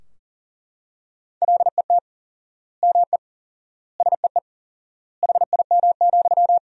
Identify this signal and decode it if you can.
Morse code, heard as a keyed tone, about 40 wpm, 700 Hz. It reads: LET ME SEE HIM,
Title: 4x Word-Spacing - Sentences from Top 400 Words - 40wpm